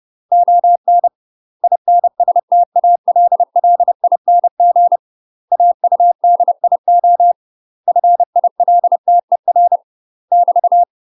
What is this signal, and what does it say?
Morse code, heard as a keyed tone, about 30 wpm, 700 Hz. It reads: ON INSTALLING AUDIO FILTER <BT>
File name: UK MASTER QSOs TRAINING / 30WPM / UK MASTER QSOs 0042